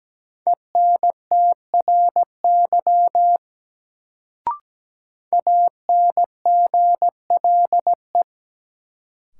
Morse code, heard as a keyed tone, about 17 wpm, 700 Hz. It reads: ENTRY ANGLE
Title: Sentences from Top 1500 Words - New Word Drill - 17wpm